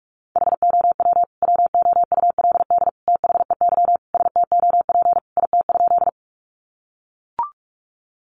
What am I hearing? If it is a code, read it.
Morse code, heard as a keyed tone, about 44 wpm, 700 Hz. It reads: HOW WOULD THEY STOP IT?
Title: Sentences from Top 300 Words 45wpm